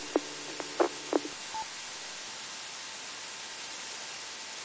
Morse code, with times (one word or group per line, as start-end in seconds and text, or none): none